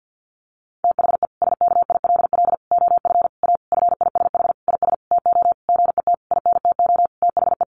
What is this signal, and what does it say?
Morse code, heard as a keyed tone, about 50 wpm, 700 Hz. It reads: THE SKILL OF A FISH IS TO GET INTO THE
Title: Sentences from Top 1400 Words 50wpm